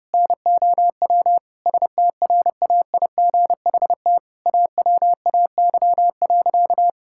Morse code, heard as a keyed tone, about 30 wpm, 700 Hz. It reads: NOW STRAIGHT AWAY.
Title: Sentences from Top 1000 Words - Review Part 6 - 30wpm